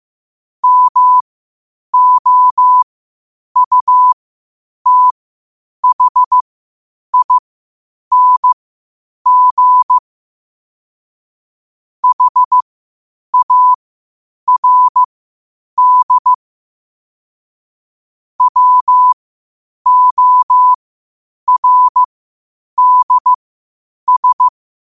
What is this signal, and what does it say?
Morse code, heard as a keyed tone, about 15 wpm, spaced as for 8 wpm, 1000 Hz. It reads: MOUTHING HARD WORDS